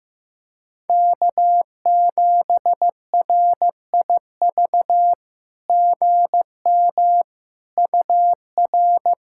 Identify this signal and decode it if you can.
Morse code, heard as a keyed tone, about 15 wpm, 700 Hz. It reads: K7RIV GM UR
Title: SOTA QSO Practice 15wpm